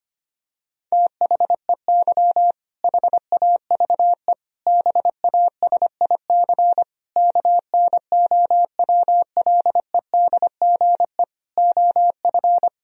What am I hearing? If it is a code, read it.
Morse code, heard as a keyed tone, about 25 wpm, 700 Hz. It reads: THEY HAVE BASIC KNOWLEDGE OF
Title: Sentences from Top 1000 Words - Review Part 3 - 25wpm